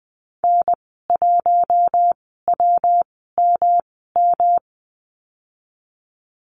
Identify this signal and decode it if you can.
Morse code, heard as a keyed tone, about 19 wpm, spaced as for 14 wpm, 700 Hz. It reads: N1WMM
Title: 167 Lesson 27g - Number 7 - Call Signs 20wpm